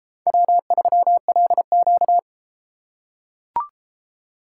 Morse code, heard as a keyed tone, about 30 wpm, 700 Hz. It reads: W3LQ